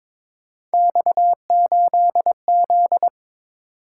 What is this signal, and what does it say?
Morse code, heard as a keyed tone, about 22 wpm, 700 Hz. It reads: X8Z
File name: practice / 22wpm / Three Letter-Number Combinations - 22wpm